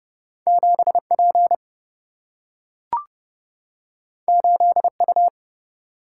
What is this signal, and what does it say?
Morse code, heard as a keyed tone, about 30 wpm, 700 Hz. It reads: 7P 8U
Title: Single-Letter Single-Number 30wpm